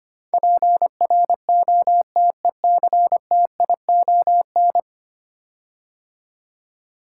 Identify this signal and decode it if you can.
Morse code, heard as a keyed tone, about 25 wpm, 700 Hz. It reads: PROTECTION
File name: 10 Letter Words - First 3 Letters Given - 25wpm